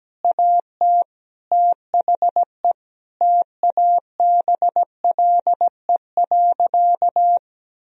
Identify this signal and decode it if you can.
Morse code, heard as a keyed tone, about 17 wpm, 700 Hz. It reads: AT THE TABLE.
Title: Sentences from Top 1000 Words - Review Part 2 - 17wpm